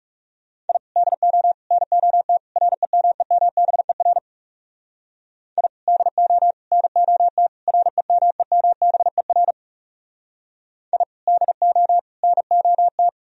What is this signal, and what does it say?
Morse code, heard as a keyed tone, about 38 wpm, 700 Hz. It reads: I DO NOT REMEMBER I DO NOT REMEMBER I DO NOT